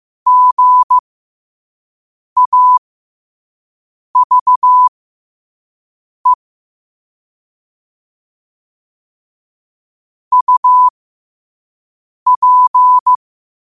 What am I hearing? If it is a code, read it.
Morse code, heard as a keyed tone, about 15 wpm, spaced as for 5 wpm, 1000 Hz. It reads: GAVE UP
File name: qotdfiles / 5wpm